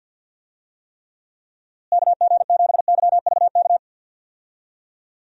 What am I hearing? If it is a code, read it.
Morse code, heard as a keyed tone, about 45 wpm, 700 Hz. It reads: KG7YVK
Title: USA Call Signs - 6 Characters - Repeated 2x - 50wpm